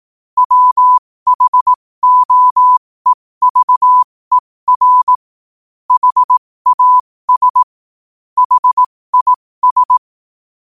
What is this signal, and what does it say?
Morse code, heard as a keyed tone, about 18 wpm, spaced as for 15 wpm, 1000 Hz. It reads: WHOEVER HAS HIS
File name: qotdfiles / 15wpm